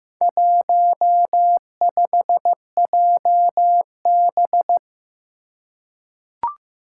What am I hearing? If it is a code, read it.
Morse code, heard as a keyed tone, about 15 wpm, 700 Hz. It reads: 15JB
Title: Four Challenging Letter-Number Combinations 15wpm